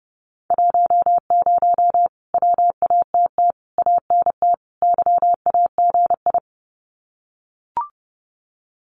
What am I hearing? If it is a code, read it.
Morse code, heard as a keyed tone, about 30 wpm, 700 Hz. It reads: 10 WATT ANT YAGI
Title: Common QSO Phrases Encore 30wpm